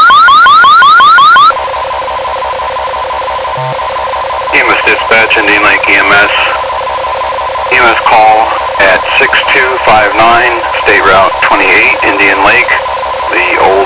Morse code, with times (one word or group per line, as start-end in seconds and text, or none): none